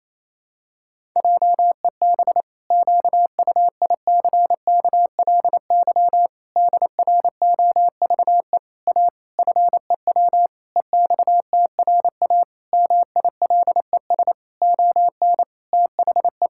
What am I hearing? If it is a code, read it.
Morse code, heard as a keyed tone, about 28 wpm, 700 Hz. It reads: JEB QUICKLY DROVE A FEW EXTRA MILES ON THE
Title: Pangram Challenge v2 - 28wpm